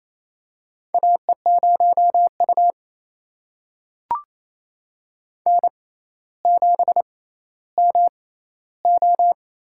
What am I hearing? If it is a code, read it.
Morse code, heard as a keyed tone, about 28 wpm, spaced as for 14 wpm, 700 Hz. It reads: AE0U N7MO
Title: USA Call Signs - 4 Characters - 6x Character Spacing - 28wpm